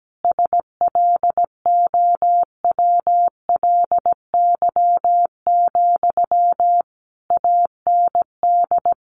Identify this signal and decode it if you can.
Morse code, heard as a keyed tone, about 17 wpm, 700 Hz. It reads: SLOWLY, AND
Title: Sentences from Top 1100 Words 17wpm